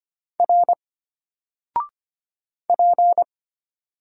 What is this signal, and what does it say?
Morse code, heard as a keyed tone, about 25 wpm, 700 Hz. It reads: R P
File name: Single-Letter-Number-25wpm